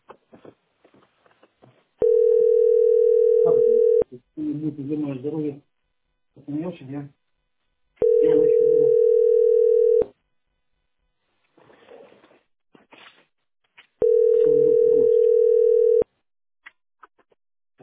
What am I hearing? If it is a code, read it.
Ringback tone (North America), 440 + 480 Hz, with speech.
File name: 36234dc0-c25b-41f7-92d6